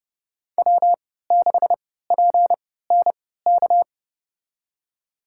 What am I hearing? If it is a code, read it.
Morse code, heard as a keyed tone, about 32 wpm, spaced as for 17 wpm, 700 Hz. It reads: W6PNK
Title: USA Call Signs - 5 Characters - 3x Character Spacing - 30wpm